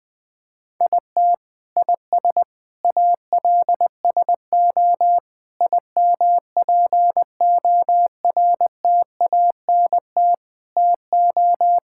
Morse code, heard as a keyed tone, about 20 wpm, 700 Hz. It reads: IT IS ALSO IMPORTANT TO
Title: Sentences with 8-Word Phrases - Spoken Beginning - 20wpm